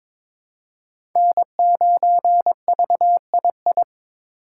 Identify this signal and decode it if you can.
Morse code, heard as a keyed tone, about 22 wpm, 700 Hz. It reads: N9VII
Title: USA Call Signs - 5 Characters - 2x Character Spacing - 22wpm